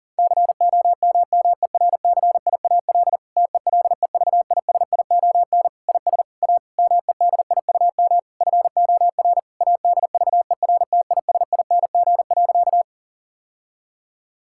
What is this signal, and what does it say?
Morse code, heard as a keyed tone, about 40 wpm, 700 Hz. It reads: COMMERCIAL TELEVISION IS A MEDIUM FOR ADVERTISING.
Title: Sentences from Top 1000 Words - Review Part 6 - 40wpm